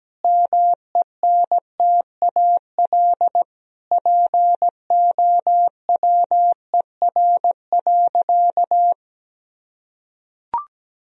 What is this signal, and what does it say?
Morse code, heard as a keyed tone, about 17 wpm, 700 Hz. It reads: MENTAL POWER.
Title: Sentences from Top 1000 Words - Review Part 4 - 17wpm